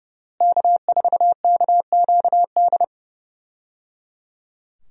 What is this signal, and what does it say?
Morse code, heard as a keyed tone, about 28 wpm, 700 Hz. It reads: K4KQD
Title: USA Call Signs - 5 Characters - Repeated 3x - 30wpm